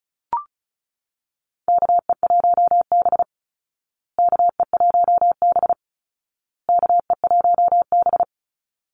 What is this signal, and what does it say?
Morse code, heard as a keyed tone, about 34 wpm, 700 Hz. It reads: KE1B KE1B KE1B
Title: Call Signs Repeated 2x - 35wpm